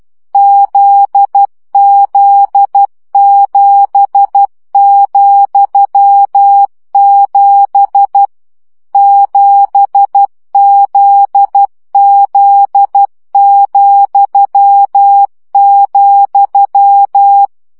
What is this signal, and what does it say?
Morse code, heard as a keyed tone, about 12 wpm, 800 Hz. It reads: ZZ7,7 7ZZ,,
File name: CW gyakorlas z 7 vesszo 10 baud